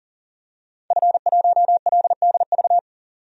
Morse code, heard as a keyed tone, about 40 wpm, 700 Hz. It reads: F1LDV
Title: DXCC-100 Call Signs - 5 Characters - 40wpm